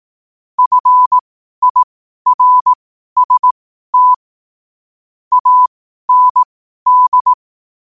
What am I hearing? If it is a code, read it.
Morse code, heard as a keyed tone, about 18 wpm, spaced as for 12 wpm, 1000 Hz. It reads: FIRST AND
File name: qotdfiles / 12wpm